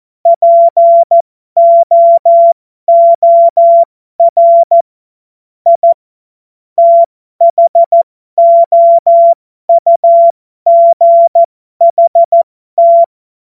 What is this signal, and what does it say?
Morse code, heard as a keyed tone, about 14 wpm, spaced as for 12 wpm, 680 Hz. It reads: POOR I THOUGHT
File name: Morsekurs / Witze